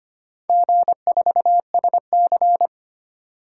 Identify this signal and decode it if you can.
Morse code, heard as a keyed tone, about 25 wpm, 700 Hz. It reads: G4SC
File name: practice / 25wpm / Four Letter-Number Combinations - 25wpm